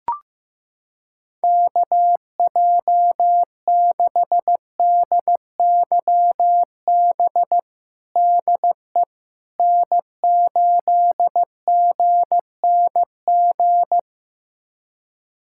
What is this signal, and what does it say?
Morse code, heard as a keyed tone, about 15 wpm, 700 Hz. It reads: KJ6DYB DE N8GNG
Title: USA Callsign DE Callsign 15wpm